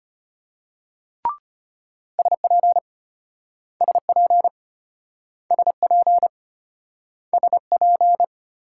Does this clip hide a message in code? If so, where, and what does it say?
Yes: Morse code from 2.2 to 8.2 s, heard as a keyed tone, about 34 wpm, 700 Hz. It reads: SP SP SP SP